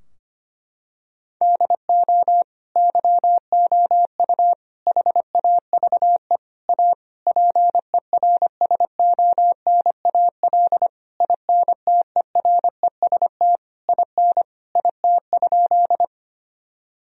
Morse code, heard as a keyed tone, about 25 wpm, 700 Hz. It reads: DO YOU HAVE A PERSONAL INTEREST IN IT?
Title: Sentences from Top 1000 Words - Review Part 4 - 25wpm